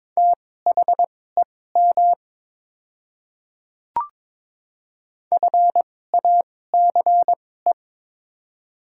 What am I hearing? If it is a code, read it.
Morse code, heard as a keyed tone, about 22 wpm, spaced as for 16 wpm, 700 Hz. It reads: THEM FACE
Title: 2x Character Spacing - Top 500 Words - 22wpm